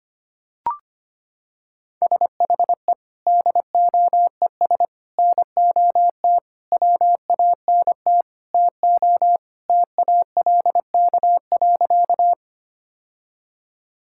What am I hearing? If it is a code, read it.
Morse code, heard as a keyed tone, about 25 wpm, 700 Hz. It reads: SHE DOES NOT WANT TO TALK.
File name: practice / 25wpm / Sentences with 4-Word Phrases - 25wpm